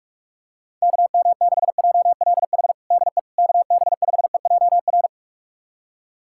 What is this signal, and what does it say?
Morse code, heard as a keyed tone, about 46 wpm, 700 Hz. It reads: KM6JLH DE KB5EJR
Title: USA Callsign DE Callsign - 6 Characters - 45wpm